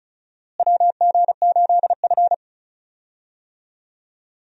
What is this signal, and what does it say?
Morse code, heard as a keyed tone, about 34 wpm, 700 Hz. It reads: WG8F
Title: USA Call Signs - 4 Characters - Repeated 2x - 35wpm